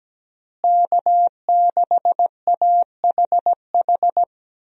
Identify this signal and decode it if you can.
Morse code, heard as a keyed tone, about 17 wpm, 700 Hz. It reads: K6AHH
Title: USA Call Signs - 5 Characters - Repeated 3x - 17wpm